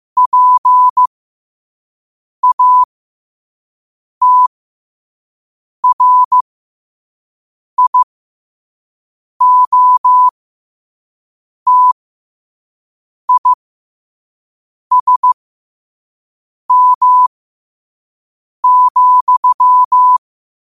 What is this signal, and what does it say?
Morse code, heard as a keyed tone, about 15 wpm, spaced as for 5 wpm, 1000 Hz. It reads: PATRIOTISM,
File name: 2026-02-15-5wpm